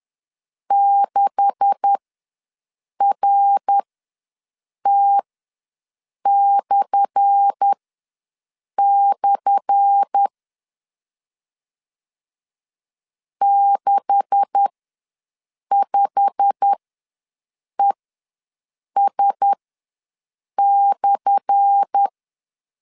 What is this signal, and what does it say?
Morse code, heard as a keyed tone, about 10 wpm, spaced as for 6 wpm, 800 Hz. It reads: 6RT// 65ES/
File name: LEZIONI / 10 SIGNS / 10A SLASH ws 10wpm 4121